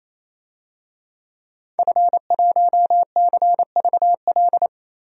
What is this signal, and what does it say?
Morse code, heard as a keyed tone, about 28 wpm, 700 Hz. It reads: F1CVL
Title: DXCC-100 Call Signs - 5 Characters - 28wpm